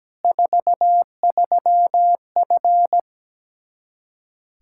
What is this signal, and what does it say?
Morse code, heard as a keyed tone, about 17 wpm, 700 Hz. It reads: 43F
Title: K1USN SST QSO Practice 17wpm